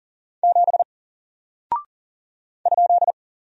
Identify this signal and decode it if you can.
Morse code, heard as a keyed tone, about 40 wpm, 700 Hz. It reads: 7 ?
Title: Single Character (Letter:Number:Punctuation:Prosign) Alt 40wpm